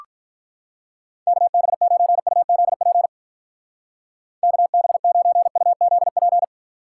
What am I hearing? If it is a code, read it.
Morse code, heard as a keyed tone, about 47 wpm, 700 Hz. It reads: KB9UZP KB9UZP